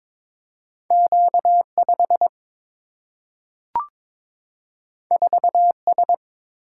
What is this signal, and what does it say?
Morse code, heard as a keyed tone, about 22 wpm, 700 Hz. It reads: Q 5 4 S
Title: Single-Letter Single-Number 22wpm